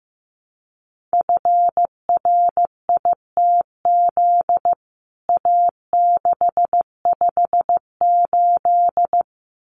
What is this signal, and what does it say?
Morse code, heard as a keyed tone, about 15 wpm, 700 Hz. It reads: FRITZ A658
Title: CWT Exchange 15wpm